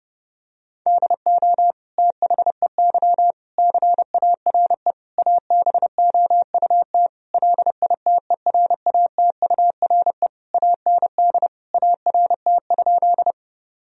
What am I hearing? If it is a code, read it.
Morse code, heard as a keyed tone, about 32 wpm, 700 Hz. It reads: DO THEY CARE ABOUT LITERATURE AND ART?
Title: Sentences from Top 1000 Words - Review Part 3 - 30wpm